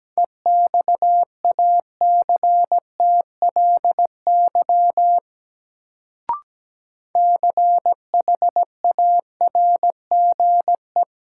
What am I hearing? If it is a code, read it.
Morse code, heard as a keyed tone, about 17 wpm, 700 Hz. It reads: EXACTLY CHARGE
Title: Sentences from Top 900 Words - New Word Drill - 17wpm